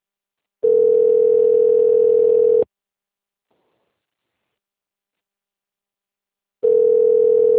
Ringback tone (North America), 440 + 480 Hz.